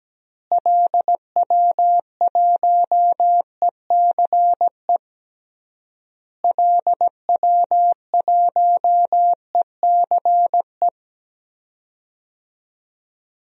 Morse code, heard as a keyed tone, about 17 wpm, 700 Hz. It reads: LW1ECE LW1ECE
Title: DXCC-100 Call Signs - 6 Characters - Repeated 3x - 17wpm